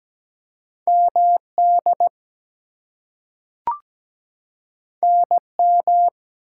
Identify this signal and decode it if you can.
Morse code, heard as a keyed tone, about 17 wpm, 700 Hz. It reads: MD NM
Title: US States and Canadian Provinces 17wpm